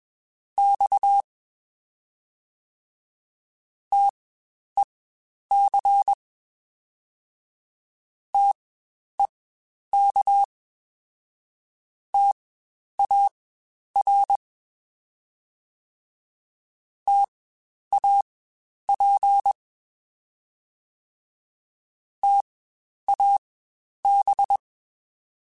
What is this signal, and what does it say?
Morse code, heard as a keyed tone, about 21 wpm, spaced as for 10 wpm, 800 Hz. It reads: X TEC TEK TAR TAP TAB
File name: Spaced Alphas / Spaced Alphas mp3s / Ts sp @20